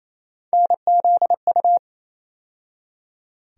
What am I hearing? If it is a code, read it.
Morse code, heard as a keyed tone, about 28 wpm, 700 Hz. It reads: NZU
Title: USA Call Sign Postfixes - 3 Letters 28wpm